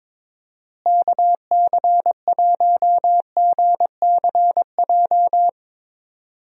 Morse code, heard as a keyed tone, about 22 wpm, 700 Hz. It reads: KC1GCJ
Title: USA Call Signs - 6 Characters - Repeated 3x - 22wpm